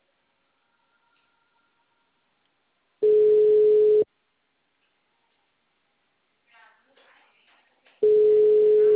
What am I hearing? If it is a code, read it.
Ringback tone (Europe, 425 Hz), with speech.
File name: out- -2009- - - .20776